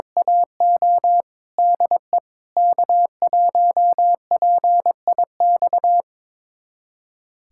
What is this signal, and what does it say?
Morse code, heard as a keyed tone, about 22 wpm, 700 Hz. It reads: AO DE K1PIX
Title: USA Callsign DE Callsign - 5 Characters - 22wpm